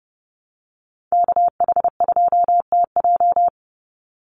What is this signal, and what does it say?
Morse code, heard as a keyed tone, about 30 wpm, 700 Hz. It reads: KH2TJ
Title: NA SOTA Chaser Callsigns 30wpm